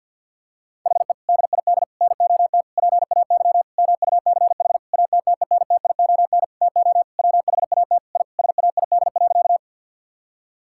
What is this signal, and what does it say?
Morse code, heard as a keyed tone, about 51 wpm, 700 Hz. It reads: HE DID NOT PAY MUCH ATTENTION TO WHAT I SAID.
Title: Sentences from Top 1000 Words - Review Part 6 - 50wpm